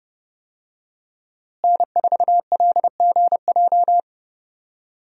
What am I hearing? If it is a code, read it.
Morse code, heard as a keyed tone, about 30 wpm, 700 Hz. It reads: N4LGJ
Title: USA Call Signs - 5 Characters - Repeated 3x - 30wpm